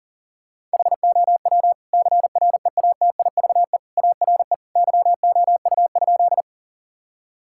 Morse code, heard as a keyed tone, about 40 wpm, 700 Hz. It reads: HOW CREATIVE ARE YOU?
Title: Sentences from Top 1000 Words - Review Part 5 - 40wpm